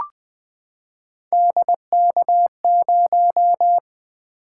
Morse code, heard as a keyed tone, about 20 wpm, 700 Hz. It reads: DK0
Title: Three Letter-Number Combinations - 20wpm